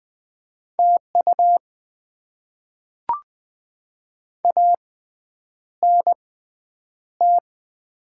Morse code, heard as a keyed tone, about 20 wpm, spaced as for 9 wpm, 690 Hz. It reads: TU ANT
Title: Top 20 QSO Elements - 6x Character Spacing - 20wpm